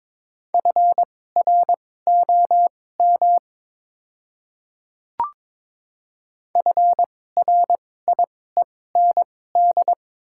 Morse code, heard as a keyed tone, about 22 wpm, spaced as for 16 wpm, 700 Hz. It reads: FROM FRIEND
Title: 2x Character Spacing - Top 300 Words - 22wpm